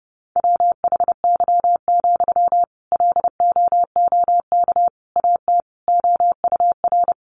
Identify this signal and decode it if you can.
Morse code, heard as a keyed tone, about 30 wpm, 700 Hz. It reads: WHY, LOOK AT OUR
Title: Sentences from Top 600 Words 30wpm